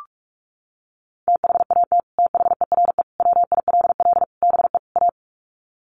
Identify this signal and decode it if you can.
Morse code, heard as a keyed tone, about 44 wpm, 700 Hz. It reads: THAT THERE WILL BE A
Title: Sets of 5 Words Encore2 45wpm